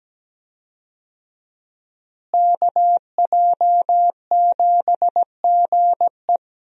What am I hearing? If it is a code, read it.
Morse code, heard as a keyed tone, about 17 wpm, 700 Hz. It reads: KJ7GE
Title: USA Call Signs - 5 Characters - Repeated 3x - 17wpm